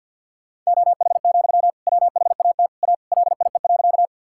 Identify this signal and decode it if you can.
Morse code, heard as a keyed tone, about 51 wpm, 700 Hz. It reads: OH, WHAT A LIE.
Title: Sentences from Top 1600 Words 50wpm